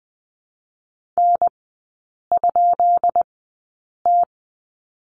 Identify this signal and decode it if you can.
Morse code, heard as a keyed tone, about 20 wpm, spaced as for 8 wpm, 700 Hz. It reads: N?T
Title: 152 Lesson 25e - Letter p - Three Characters 20wpm